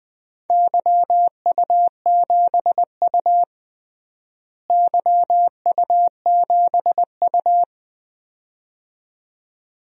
Morse code, heard as a keyed tone, about 20 wpm, 700 Hz. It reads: YU7U YU7U
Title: DXCC-100 Call Signs - 4 Characters - Repeated 2x - 20wpm